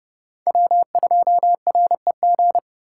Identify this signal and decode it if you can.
Morse code, heard as a keyed tone, about 32 wpm, 700 Hz. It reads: W2REG
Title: USA Call Signs - 5 Characters - 2x Character Spacing - 30wpm